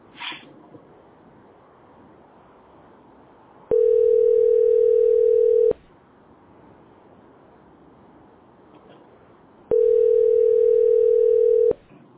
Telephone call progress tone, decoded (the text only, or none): ringback tone